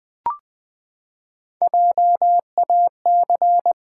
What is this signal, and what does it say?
Morse code, heard as a keyed tone, about 20 wpm, 700 Hz. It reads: JAC
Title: Build A Word - 3 Letters At A Time - 6-Letter Words - 20wpm